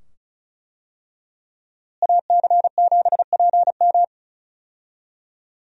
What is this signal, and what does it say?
Morse code, heard as a keyed tone, about 36 wpm, 700 Hz. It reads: AC7PM
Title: USA Call Signs - 5 Characters - Repeated 2x - 35wpm